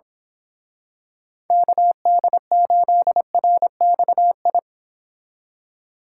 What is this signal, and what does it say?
Morse code, heard as a keyed tone, about 26 wpm, 700 Hz. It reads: KD8RXI
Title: Speed Racing - USA 2x3 Call Signs 22wpm